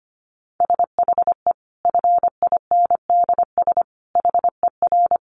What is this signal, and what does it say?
Morse code, heard as a keyed tone, about 25 wpm, 700 Hz. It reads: SHE FINDS HER
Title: Additional Easy Phrases 25wpm